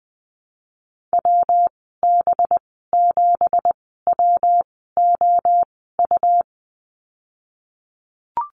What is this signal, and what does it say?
Morse code, heard as a keyed tone, about 20 wpm, spaced as for 15 wpm, 700 Hz. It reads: WB7WOU
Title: 186 Lesson 30g - Letter b - Call Signs 20wpm